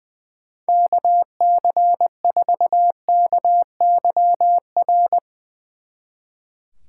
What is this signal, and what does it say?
Morse code, heard as a keyed tone, about 21 wpm, 700 Hz. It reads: KC4KYR
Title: Speed Racing - USA 2x3 Call Signs 20wpm